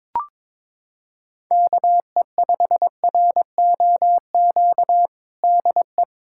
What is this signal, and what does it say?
Morse code, heard as a keyed tone, about 22 wpm, 700 Hz. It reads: KE5ROQ DE
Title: USA Callsign DE Callsign - 6 Characters - 22wpm